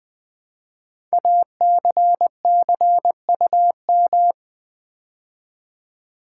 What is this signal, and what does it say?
Morse code, heard as a keyed tone, about 20 wpm, 700 Hz. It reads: ACCUM
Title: Build A Word - 5 Letters At A Time - 12-Letter Words - 20wpm